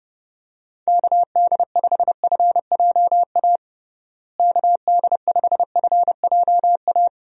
Morse code, heard as a keyed tone, about 32 wpm, 700 Hz. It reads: KD5FJA KD5FJA
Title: USA Call Signs - 6 Characters - Repeated 2x - 30wpm